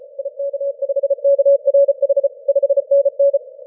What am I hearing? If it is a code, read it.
Morse code, heard as a keyed tone, about 36 wpm, 560 Hz. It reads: IK5KRH 5NN
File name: wp / yw5b 30m cw 25 aug 2012